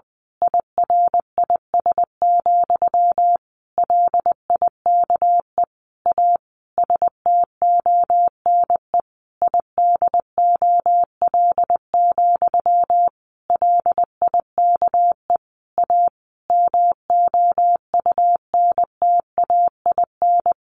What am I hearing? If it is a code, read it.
Morse code, heard as a keyed tone, about 20 wpm, 700 Hz. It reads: IRIS, LIKE A STONE IDOL, LIKE A MOUNTAIN
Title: A Sound Of Thunder 20wpm